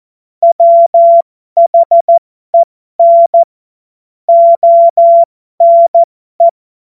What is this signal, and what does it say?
Morse code, heard as a keyed tone, about 14 wpm, spaced as for 12 wpm, 680 Hz. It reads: WHEN ONE
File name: Morsekurs / Witze